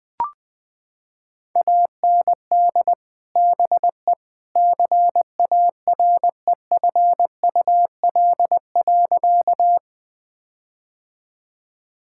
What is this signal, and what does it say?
Morse code, heard as a keyed tone, about 21 wpm, 700 Hz. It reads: AND BE CAREFUL.
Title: Sentences from Top 1600 Words 20wpm